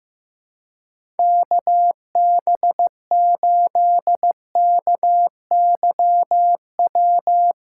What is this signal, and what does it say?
Morse code, heard as a keyed tone, about 15 wpm, 700 Hz. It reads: KB8KYW
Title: USA Callsign DE Callsign 15wpm